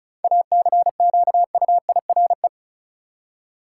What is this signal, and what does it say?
Morse code, heard as a keyed tone, about 34 wpm, 700 Hz. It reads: ACQUIRE
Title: Seven Letter Words - 35wpm